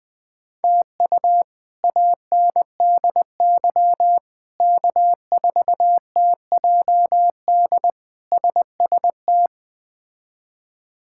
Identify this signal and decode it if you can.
Morse code, heard as a keyed tone, about 21 wpm, 700 Hz. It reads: TU ANDY K4TJD SST
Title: K1USN SST QSO Practice - 20wpm